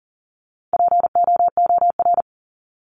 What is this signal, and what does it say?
Morse code, heard as a keyed tone, about 40 wpm, 700 Hz. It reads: POOR
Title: Top 751-1000 Words 40wpm